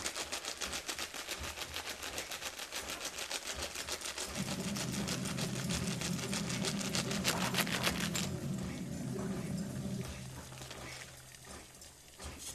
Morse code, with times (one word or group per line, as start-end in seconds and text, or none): none